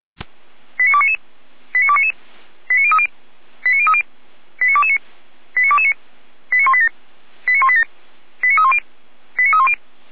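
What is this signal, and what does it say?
Five-tone selective call (ZVEI), heard as quick runs of five tones, tones of about 70 ms. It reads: 89299 89299 89030 89030 89229 89229 89187 89187 89310 89310
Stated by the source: ZVEI five-tone selective call 89075 received on SDR.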